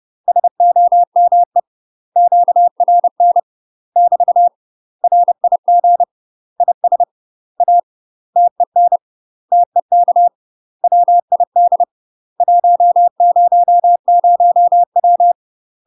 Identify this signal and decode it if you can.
Morse code, heard as a keyed tone, about 30 wpm, 700 Hz. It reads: SOME QRN <BT> RIG IS A TEN TEK WID 100W
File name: UK MASTER QSOs TRAINING / 30WPM / UK MASTER QSOs 0004